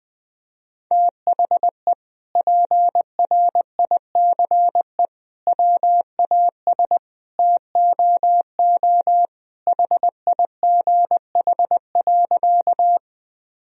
Morse code, heard as a keyed tone, about 20 wpm, 700 Hz. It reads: THE PRICE WAS TOO HIGH.
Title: Sentences from Top 1000 Words - Review Part 4 - 20wpm